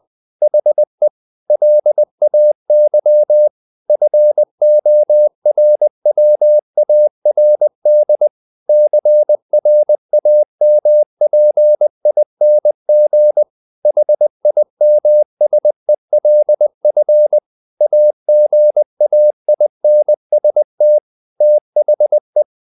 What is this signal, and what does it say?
Morse code, heard as a keyed tone, about 20 wpm, 600 Hz. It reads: HE LAY FORWARD CRAMPING HIMSELF AGAINST THE
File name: the old man and the sea part 2 20wpm